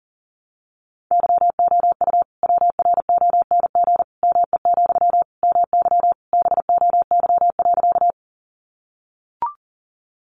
Morse code, heard as a keyed tone, about 40 wpm, 700 Hz. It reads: YOU WRONG ME, MY BOY.
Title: Sentences from Top 900 Words 40wpm